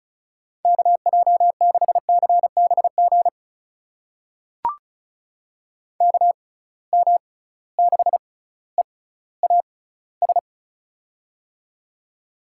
Morse code, heard as a keyed tone, about 36 wpm, spaced as for 18 wpm, 700 Hz. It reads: KJ6CBG KM6EAS
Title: USA Call Signs - 6 Characters - 6x Character Spacing - 35wpm